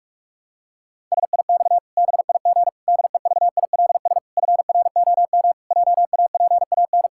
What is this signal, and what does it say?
Morse code, heard as a keyed tone, about 46 wpm, 700 Hz. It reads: SIX BIG DEVILS FROM JAPAN
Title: Pangram Challenge v2 - 45wpm